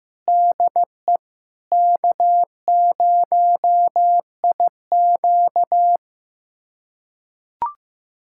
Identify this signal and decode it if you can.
Morse code, heard as a keyed tone, about 15 wpm, 700 Hz. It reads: DE K0IQ
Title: USA Callsign DE Callsign - 4 Characters - 15wpm